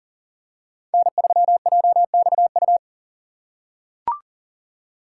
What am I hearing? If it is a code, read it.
Morse code, heard as a keyed tone, about 40 wpm, 700 Hz. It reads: N3JXU
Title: USA Call Signs - 5 Characters - 4x Character Spacing - 40wpm